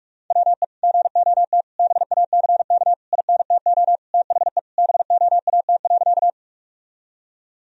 Morse code, heard as a keyed tone, about 46 wpm, 700 Hz. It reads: WE GOT BACK INTO THE BOAT.
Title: Sentences from Top 1000 Words - Review Part 3 - 45wpm